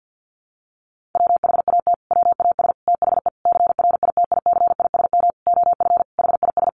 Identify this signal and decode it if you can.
Morse code, heard as a keyed tone, about 48 wpm, 700 Hz. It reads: WHAT WAS THE CRITICISM OF HIS
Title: Sentences from Top 1300 Words 50wpm